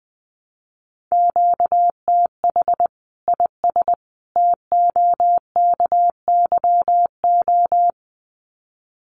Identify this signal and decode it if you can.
Morse code, heard as a keyed tone, about 20 wpm, 700 Hz. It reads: QTH IS TOKYO
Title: Common QSO Phrases 20wpm